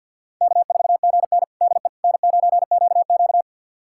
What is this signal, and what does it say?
Morse code, heard as a keyed tone, about 45 wpm, 700 Hz. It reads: K4GN DE N9QQ
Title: USA Callsign DE Callsign - 4 Characters - 50wpm